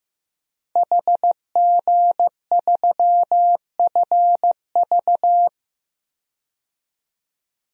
Morse code, heard as a keyed tone, about 15 wpm, 700 Hz. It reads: HG3FV
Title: DXCC-100 Call Signs - 5 Characters - Repeated 3x - 15wpm